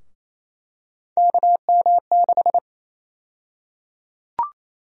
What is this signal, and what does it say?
Morse code, heard as a keyed tone, about 27 wpm, 700 Hz. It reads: KM6
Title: USA Call Sign Prefixes - Repeated 2x - 28wpm